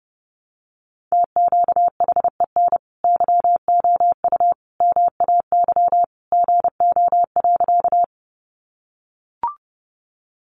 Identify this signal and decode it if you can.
Morse code, heard as a keyed tone, about 30 wpm, 700 Hz. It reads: TQHEN YOU MAY GO.
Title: Sentences from Top 100 Words 30wpm